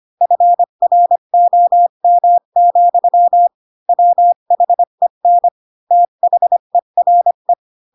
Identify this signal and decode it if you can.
Morse code, heard as a keyed tone, about 25 wpm, 700 Hz. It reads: FROM, WHEN THERE
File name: Pinocchio / Pinocchio ch1 20 25 ik1hge